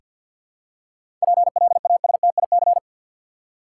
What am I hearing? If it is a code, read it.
Morse code, heard as a keyed tone, about 48 wpm, 700 Hz. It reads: PLASTIC
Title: Seven Letter Words - 50wpm